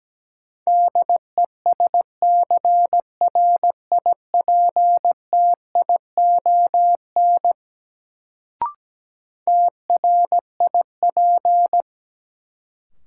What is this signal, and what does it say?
Morse code, heard as a keyed tone, about 17 wpm, 700 Hz. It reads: DESCRIPTION TRIP
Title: Sentences from Top 1300 Words - New Word Drill Alt - 17wpm